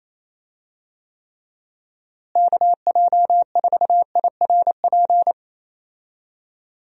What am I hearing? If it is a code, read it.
Morse code, heard as a keyed tone, about 28 wpm, 700 Hz. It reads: KJ4IRP
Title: USA Call Signs - 6 Characters - Repeated 3x - 28wpm